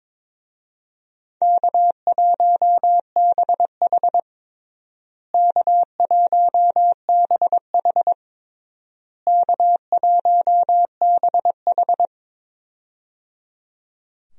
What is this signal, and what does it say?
Morse code, heard as a keyed tone, about 22 wpm, 700 Hz. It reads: K1BH K1BH K1BH
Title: USA Call Signs - 4 Characters - Repeated 3x - 22wpm